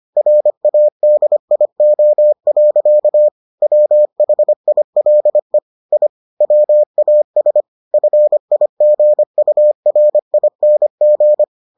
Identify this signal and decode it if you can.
Morse code, heard as a keyed tone, about 25 wpm, 600 Hz. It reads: RADIO. WHILE I WAS FIGURING